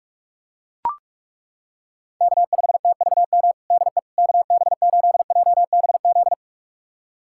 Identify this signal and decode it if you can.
Morse code, heard as a keyed tone, about 41 wpm, 700 Hz. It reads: K5TVM DE KB8JBZ
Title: USA Callsign DE Callsign 45wpm